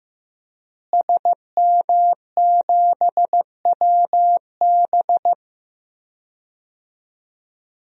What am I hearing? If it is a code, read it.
Morse code, heard as a keyed tone, about 15 wpm, 700 Hz. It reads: SM7WB
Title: DXCC-100 Call Signs - 5 Characters - Repeated 2x - 15wpm